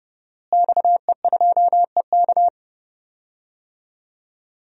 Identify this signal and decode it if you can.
Morse code, heard as a keyed tone, about 32 wpm, 700 Hz. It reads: XE2EK
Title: DXCC-100 Call Signs - 5 Characters - Repeated 3x - 30wpm